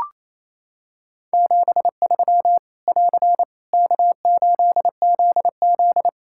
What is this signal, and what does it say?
Morse code, heard as a keyed tone, about 27 wpm, 700 Hz. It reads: 73 <AR> K8ZZ
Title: K1USN SST QSO Practice 28wpm